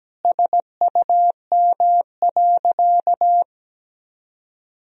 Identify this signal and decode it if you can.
Morse code, heard as a keyed tone, about 17 wpm, 700 Hz. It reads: SUM.
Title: Sentences from Top 100 Words and Abbreviations 17wpm